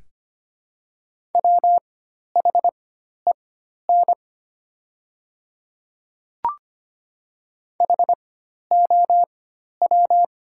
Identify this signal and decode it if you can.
Morse code, heard as a keyed tone, about 25 wpm, spaced as for 12 wpm, 700 Hz. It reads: WHEN HOW
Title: 4x Character Spacing - Top 200 Words - 25wpm